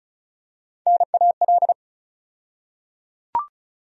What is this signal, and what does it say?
Morse code, heard as a keyed tone, about 34 wpm, 700 Hz. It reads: NAL
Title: USA Call Sign Postfixes - 3 Letters 35wpm